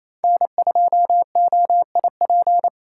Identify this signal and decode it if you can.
Morse code, heard as a keyed tone, about 28 wpm, 700 Hz. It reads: N2OIP
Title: USA Call Signs - 5 Characters - 2x Character Spacing - 28wpm